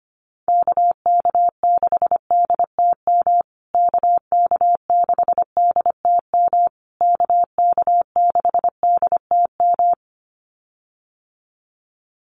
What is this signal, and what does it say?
Morse code, heard as a keyed tone, about 25 wpm, 700 Hz. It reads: KK6DTM KK6DTM KK6DTM
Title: USA Call Signs Repeated 3x 25wpm